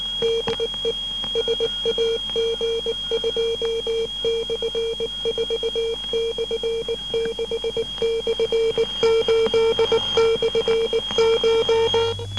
Morse code, heard as a keyed tone, about 19 wpm, 460 Hz, with noise. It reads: DE SAG2/4/6/8/CH
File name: SAG6 12880.5 1218